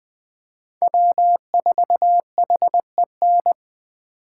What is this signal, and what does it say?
Morse code, heard as a keyed tone, about 20 wpm, 700 Hz. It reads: W4HEN
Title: Speed Racing - USA 1x3 Call Signs 17wpm